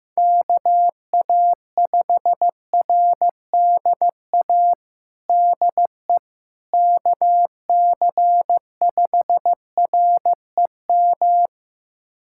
Morse code, heard as a keyed tone, about 15 wpm, 700 Hz. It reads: KA5RDA DE KC5REM
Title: USA Callsign DE Callsign 15wpm